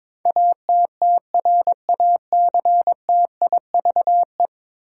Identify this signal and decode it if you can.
Morse code, heard as a keyed tone, about 22 wpm, 700 Hz. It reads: ATTRACTIVE
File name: practice / 22wpm / 10 Letter Words - First 3 Letters Given - 22wpm